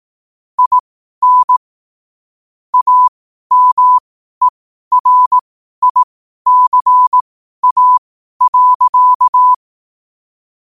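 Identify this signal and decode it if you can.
Morse code, heard as a keyed tone, about 18 wpm, spaced as for 12 wpm, 1000 Hz. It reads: IN AMERICA.
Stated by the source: Morse code quote practice at 12 words per minute.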